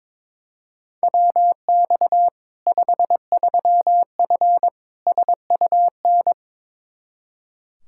Morse code, heard as a keyed tone, about 22 wpm, 700 Hz. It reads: WX 53F SUN
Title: Practice QSO - Exchange 2 - Info - 22wpm